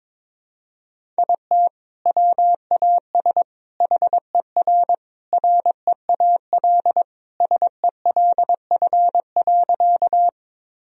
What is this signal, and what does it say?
Morse code, heard as a keyed tone, about 22 wpm, 700 Hz. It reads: IT WAS HER REAL SELF.
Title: Sentences from Top 1000 Words - Review Part 1 - 22wpm